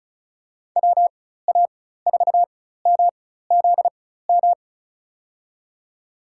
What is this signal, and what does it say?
Morse code, heard as a keyed tone, about 36 wpm, spaced as for 16 wpm, 700 Hz. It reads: WA4MZM
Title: USA Call Signs - 6 Characters - 4x Character Spacing - 35wpm